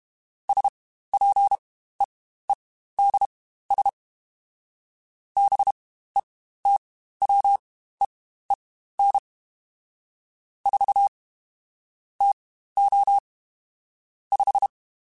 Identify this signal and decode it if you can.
Morse code, heard as a keyed tone, about 32 wpm, spaced as for 15 wpm, 800 Hz. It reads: SPEEDS BETWEEN 4 TO 5
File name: Spaced Characters